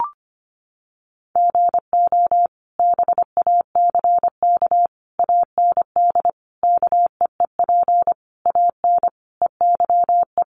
Morse code, heard as a keyed tone, about 25 wpm, 700 Hz. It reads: GO BACK AND KEEP AN EYE
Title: Sentences from Top 300 Words 25wpm